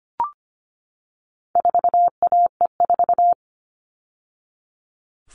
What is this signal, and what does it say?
Morse code, heard as a keyed tone, about 25 wpm, 700 Hz. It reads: 4AE4
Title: All Cut Numbers - 4-Digit 25wpm